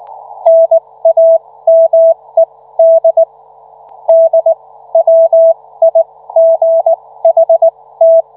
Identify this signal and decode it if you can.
Morse code, heard as a keyed tone, about 19 wpm, spaced as for 15 wpm, 660 Hz. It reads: NAMED DWIGHT